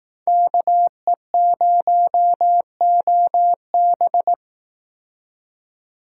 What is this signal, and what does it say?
Morse code, heard as a keyed tone, about 18 wpm, 700 Hz. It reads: KE0OB
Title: Speed Racing - USA 2x2 Call Signs 15wpm